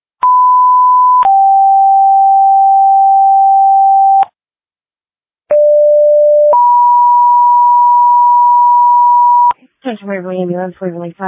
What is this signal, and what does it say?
Two-tone page, heard as a one-second tone, then a three-second tone, about 1000 then 780 Hz, about 590 then 970 Hz, with speech. It is heard from 0.2 to 9.5 s.